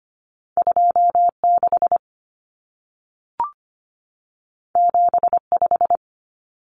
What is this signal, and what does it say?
Morse code, heard as a keyed tone, about 25 wpm, 700 Hz. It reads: 26 75
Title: Numbers - 2-Digit 25wpm